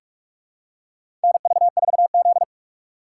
Morse code, heard as a keyed tone, about 45 wpm, 700 Hz. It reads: NV4Z
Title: USA Call Signs - 4 Characters - 6x Character Spacing - 45wpm